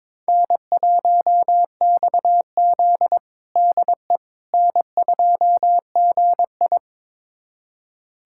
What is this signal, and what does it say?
Morse code, heard as a keyed tone, about 22 wpm, 700 Hz. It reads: N1XZ DE N2GI
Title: USA Callsign DE Callsign - 4 Characters - 22wpm